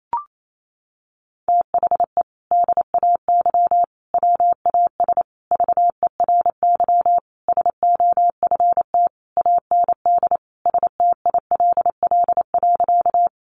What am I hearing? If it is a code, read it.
Morse code, heard as a keyed tone, about 28 wpm, 700 Hz. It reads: THE DAY WAS VERY SOFT AND STILL.
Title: Sentences from Top 1500 Words 28wpm